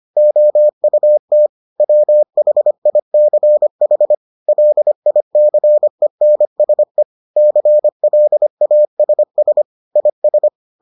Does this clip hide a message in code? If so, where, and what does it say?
Yes: Morse code from 0.2 to 10.5 s, heard as a keyed tone, about 25 wpm, 600 Hz. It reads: OUT WHICH LICENSE CLASS IS